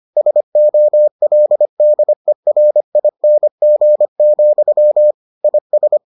Morse code, heard as a keyed tone, about 25 wpm, 600 Hz. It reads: SOLDERING, IS